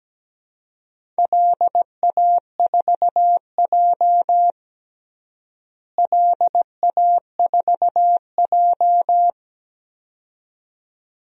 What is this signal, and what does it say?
Morse code, heard as a keyed tone, about 17 wpm, 700 Hz. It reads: LA4J LA4J
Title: DXCC-100 Call Signs - 4 Characters - Repeated 2x - 17wpm